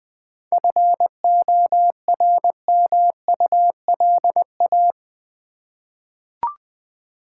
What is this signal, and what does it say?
Morse code, heard as a keyed tone, about 21 wpm, 690 Hz. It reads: FORMULA
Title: 7 Letter Words - First 3 Letters Given - 20wpm